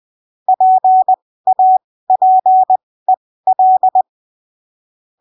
Morse code, heard as a keyed tone, about 20 wpm, spaced as for 15 wpm, 750 Hz. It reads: PAPEL
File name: 20ppm